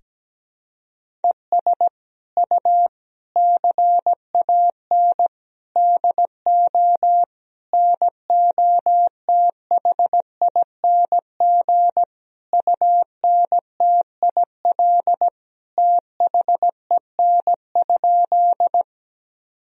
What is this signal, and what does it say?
Morse code, heard as a keyed tone, about 17 wpm, 700 Hz. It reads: ES U CAN DO NOTHING UNTIL THEN?
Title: Sentences from Top 500 Words and Abbreviations 17wpm